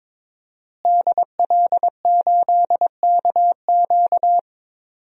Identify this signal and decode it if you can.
Morse code, heard as a keyed tone, about 22 wpm, 700 Hz. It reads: DL8KQ
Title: DXCC-100 Call Signs - 5 Characters - Repeated 3x - 22wpm